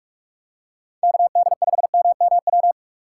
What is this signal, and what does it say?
Morse code, heard as a keyed tone, about 44 wpm, 700 Hz. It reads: KD5MMW